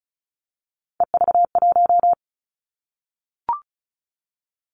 Morse code, heard as a keyed tone, about 36 wpm, 700 Hz. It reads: EV1
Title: Three Challenging Letter-Number Combinations 35wpm